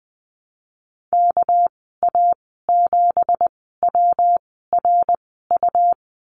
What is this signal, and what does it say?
Morse code, heard as a keyed tone, about 20 wpm, spaced as for 14 wpm, 700 Hz. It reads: KA7WRU
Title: 245 Lesson 39g - Letter z - Call Signs 20wpm